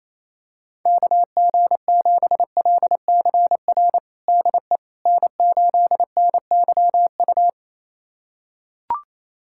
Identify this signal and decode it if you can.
Morse code, heard as a keyed tone, about 28 wpm, 700 Hz. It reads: KG7LCR DE N8NYU
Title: USA Callsign DE Callsign 28wpm